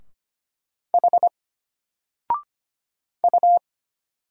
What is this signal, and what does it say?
Morse code, heard as a keyed tone, about 25 wpm, 700 Hz. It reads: H U
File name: Single-Letter-Number-25wpm